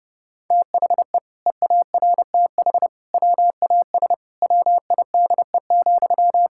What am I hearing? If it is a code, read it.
Morse code, heard as a keyed tone, about 30 wpm, 700 Hz. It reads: THE EARTH WAS WIDE,
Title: Sentences from Top 1000 Words - Review Part 6 - 30wpm